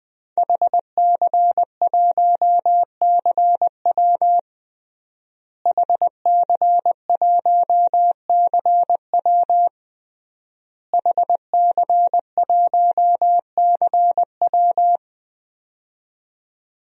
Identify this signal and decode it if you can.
Morse code, heard as a keyed tone, about 21 wpm, 700 Hz. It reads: HC1CW HC1CW HC1CW
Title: DXCC-100 Call Signs - 5 Characters - Repeated 3x - 20wpm